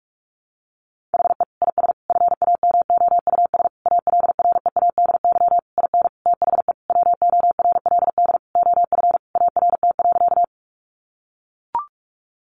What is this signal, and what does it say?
Morse code, heard as a keyed tone, about 44 wpm, 700 Hz. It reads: HE IS FAMOUS ALREADY IN THE WORLD OF ART.
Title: Sentences from Top 1500 Words 45wpm